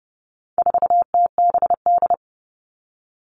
Morse code, heard as a keyed tone, about 29 wpm, 700 Hz. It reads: 4TBD
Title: All Cut Numbers - 4-Digit 30wpm